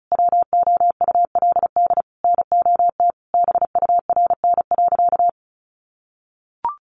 Morse code, heard as a keyed tone, about 34 wpm, 700 Hz. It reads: WOULD NOT BURN.
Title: Sentences from Top 1300 Words 35wpm